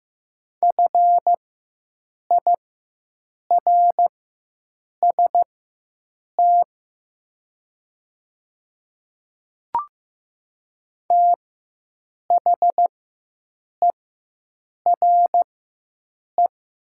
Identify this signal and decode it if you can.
Morse code, heard as a keyed tone, about 15 wpm, spaced as for 7 wpm, 700 Hz. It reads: FIRST THERE
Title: 4x Character Spacing - Top 100 Words - 15wpm